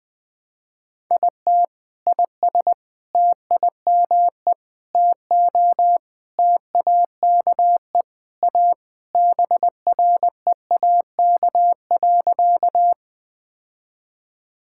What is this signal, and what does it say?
Morse code, heard as a keyed tone, about 21 wpm, 690 Hz. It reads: IT IS TIME TO TAKE A BREAK.
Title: Sentences with 4-Word Phrases - Spoken Beginning - 20wpm